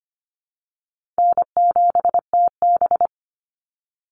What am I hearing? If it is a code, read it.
Morse code, heard as a keyed tone, about 25 wpm, 700 Hz. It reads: N7TB
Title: Easy Call Signs Repeated 3x - 25wpm